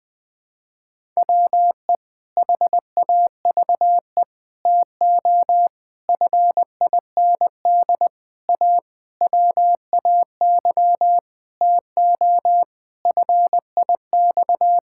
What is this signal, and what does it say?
Morse code, heard as a keyed tone, about 20 wpm, 700 Hz. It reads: WE HAVE TO FIND A WAY TO FIX
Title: Sentences with 7-Word Phrases - 20wpm